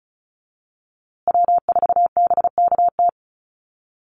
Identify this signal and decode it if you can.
Morse code, heard as a keyed tone, about 34 wpm, 700 Hz. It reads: W4BKT
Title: USA Call Signs Repeated 2x 35wpm